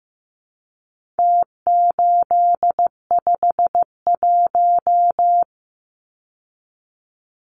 Morse code, heard as a keyed tone, about 15 wpm, 700 Hz. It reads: T851
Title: Numbers - Contest Style - 4-Digit 15wpm